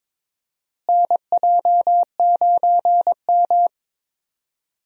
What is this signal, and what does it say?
Morse code, heard as a keyed tone, about 22 wpm, 700 Hz. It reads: NJ9M
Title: USA Call Signs - 4 Characters - Repeated 3x - 22wpm